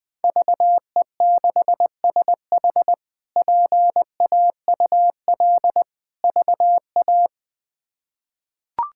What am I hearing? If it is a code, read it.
Morse code, heard as a keyed tone, about 20 wpm, 700 Hz. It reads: VE6SH PAUL VA
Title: K1USN SST QSO Practice - 20wpm